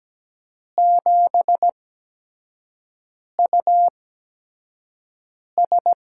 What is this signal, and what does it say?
Morse code, heard as a keyed tone, about 17 wpm, spaced as for 5 wpm, 700 Hz. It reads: 7US
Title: USA Call Signs - 4 Characters - 8x Character Spacing - 17wpm